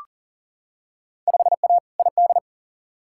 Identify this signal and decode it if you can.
Morse code, heard as a keyed tone, about 40 wpm, 700 Hz. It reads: 5A ID
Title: Field Day Exchanges Encore - 40wpm